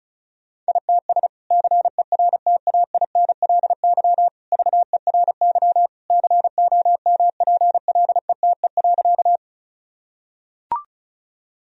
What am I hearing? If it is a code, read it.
Morse code, heard as a keyed tone, about 36 wpm, 700 Hz. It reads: ITS CERTAINLY VERY COMPLETE.
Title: Sentences from Top 1000 Words - Review Part 2 - 35wpm